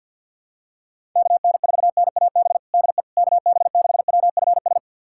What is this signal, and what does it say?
Morse code, heard as a keyed tone, about 48 wpm, 700 Hz. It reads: KN4NAB DE KB6WFS
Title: USA Callsign DE Callsign - 6 Characters - 50wpm